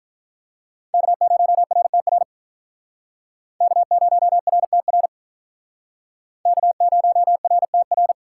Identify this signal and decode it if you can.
Morse code, heard as a keyed tone, about 44 wpm, 700 Hz. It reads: K0RTR K0RTR K0RTR